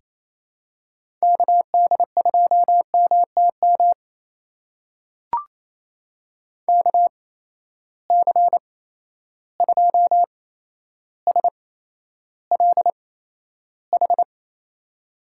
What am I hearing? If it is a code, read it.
Morse code, heard as a keyed tone, about 28 wpm, spaced as for 12 wpm, 700 Hz. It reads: KD2MTM KC2SLH
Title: USA Call Signs - 6 Characters - 8x Character Spacing - 28wpm